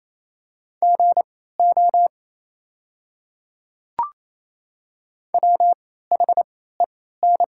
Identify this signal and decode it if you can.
Morse code, heard as a keyed tone, about 28 wpm, spaced as for 16 wpm, 700 Hz. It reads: GO WHEN
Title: 3x Character Spacing - Top 100 Words - 28wpm